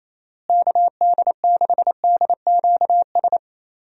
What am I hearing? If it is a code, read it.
Morse code, heard as a keyed tone, about 28 wpm, 700 Hz. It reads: KD6DQS
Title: USA Call Signs - 6 Characters - Repeated 2x - 28wpm